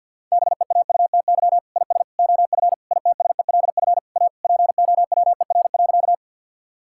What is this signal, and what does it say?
Morse code, heard as a keyed tone, about 51 wpm, 690 Hz. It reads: BEAUTY IS OF ITSELF A POWER.
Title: Sentences from Top 1000 Words - Review Part 5 - 50wpm